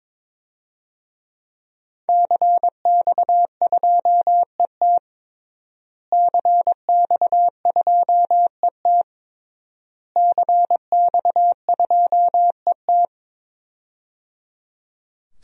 Morse code, heard as a keyed tone, about 22 wpm, 700 Hz. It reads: CX2ET CX2ET CX2ET
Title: DXCC-100 Call Signs - 5 Characters - Repeated 3x - 22wpm